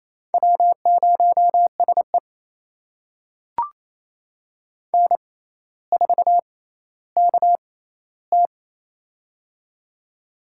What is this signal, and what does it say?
Morse code, heard as a keyed tone, about 28 wpm, spaced as for 14 wpm, 700 Hz. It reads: W0SE N4KT
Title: USA Call Signs - 4 Characters - 6x Character Spacing - 28wpm